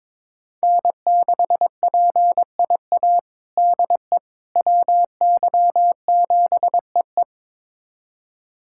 Morse code, heard as a keyed tone, about 22 wpm, 700 Hz. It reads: N6PIA DE WY7EE
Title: USA Callsign DE Callsign - 5 Characters - 22wpm